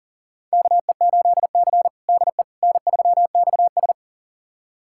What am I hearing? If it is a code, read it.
Morse code, heard as a keyed tone, about 40 wpm, 700 Hz. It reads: KE8C DE N3XS
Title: USA Callsign DE Callsign - 4 Characters - 40wpm